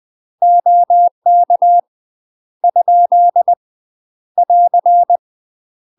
Morse code, heard as a keyed tone, about 20 wpm, 700 Hz. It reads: OK ? <AR>
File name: UK MASTER QSOs TRAINING / 20WPM / UK MASTER QSOs 0072